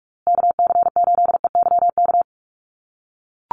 Morse code, heard as a keyed tone, about 44 wpm, 700 Hz. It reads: KC8EYK
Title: USA Call Signs Repeated 2x 45wpm